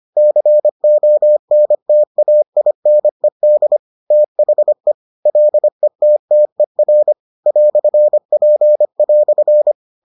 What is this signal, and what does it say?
Morse code, heard as a keyed tone, about 25 wpm, 600 Hz. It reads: CONTAINED THE LETTER "P"